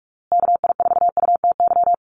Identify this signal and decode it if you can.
Morse code, heard as a keyed tone, about 44 wpm, 700 Hz. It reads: KI4UTY
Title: USA Call Signs Repeated 2x 45wpm